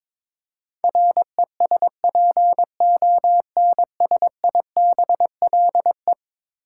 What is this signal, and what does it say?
Morse code, heard as a keyed tone, about 22 wpm, 700 Hz. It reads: RESPONSIBLE
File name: practice / 22wpm / Eleven Letter Words - 22wpm